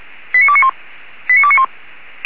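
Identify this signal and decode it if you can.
Five-tone selective call (ZVEI), heard as quick runs of five tones, tones of about 70 ms. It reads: 89291 89291
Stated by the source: ZVEI five-tone selective call received with an SDR.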